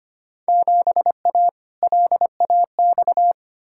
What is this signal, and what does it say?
Morse code, heard as a keyed tone, about 25 wpm, 700 Hz. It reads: 7A LAX
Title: Field Day Exchanges - 25wpm